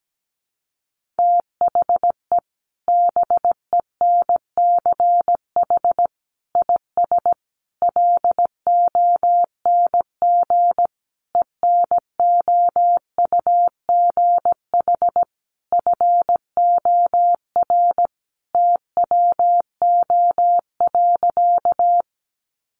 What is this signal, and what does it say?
Morse code, heard as a keyed tone, about 17 wpm, 700 Hz. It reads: THE BENCH IS LONG ENOUGH FOR TWO.
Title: Sentences from Top 1400 Words 17wpm